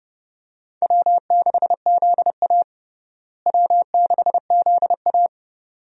Morse code, heard as a keyed tone, about 30 wpm, 700 Hz. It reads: W6ZA W6ZA
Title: USA Call Signs - 4 Characters - Repeated 2x - 30wpm